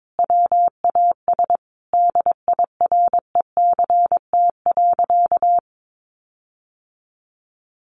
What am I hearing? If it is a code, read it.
Morse code, heard as a keyed tone, about 22 wpm, 700 Hz. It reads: WAS DIRECT.
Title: Sentences from Top 1500 Words 22wpm